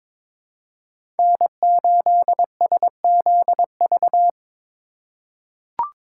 Morse code, heard as a keyed tone, about 22 wpm, 700 Hz. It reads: N8SZV
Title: USA Call Signs - 5 Characters - 3x Character Spacing - 22wpm